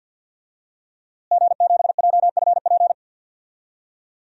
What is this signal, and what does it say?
Morse code, heard as a keyed tone, about 51 wpm, 700 Hz. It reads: G7JFP
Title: DXCC-100 Call Signs - 5 Characters - Repeated 2x - 50wpm